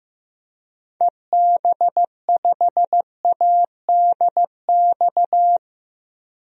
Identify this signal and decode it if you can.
Morse code, heard as a keyed tone, about 15 wpm, 700 Hz. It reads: EB5ADX
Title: DXCC-100 Call Signs - 6 Characters - Repeated 2x - 15wpm